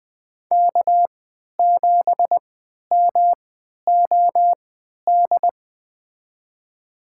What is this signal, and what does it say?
Morse code, heard as a keyed tone, about 20 wpm, spaced as for 11 wpm, 700 Hz. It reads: K7MOD
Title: USA Call Signs - 5 Characters - 3x Character Spacing - 20wpm